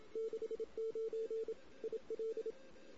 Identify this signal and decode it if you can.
Morse code, heard as a keyed tone, about 27 wpm, 450 Hz. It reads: 69 IL
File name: CW / 2008-ss-cw-k5tr-op-n5rz -16:00